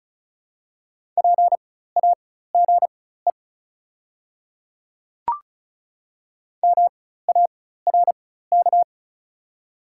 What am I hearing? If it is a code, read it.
Morse code, heard as a keyed tone, about 36 wpm, spaced as for 16 wpm, 700 Hz. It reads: PAGE MARK
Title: 4x Character Spacing - Top 300 Words - 35wpm